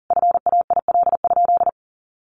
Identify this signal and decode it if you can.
Morse code, heard as a keyed tone, about 40 wpm, 700 Hz. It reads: FAIL?
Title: Sentences from Top 1100 Words 40wpm